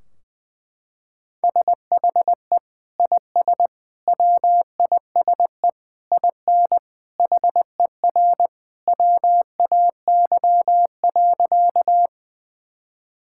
Morse code, heard as a keyed tone, about 20 wpm, 700 Hz. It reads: SHE IS WISE IN HER WAY.
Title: Sentences from Top 1600 Words 20wpm